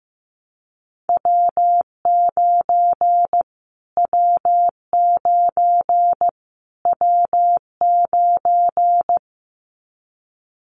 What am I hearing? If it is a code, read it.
Morse code, heard as a keyed tone, about 15 wpm, 700 Hz. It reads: W9 W9 W9
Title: Callsign Prefixes for Top 25 DXCC - Repeated 3x - 15wpm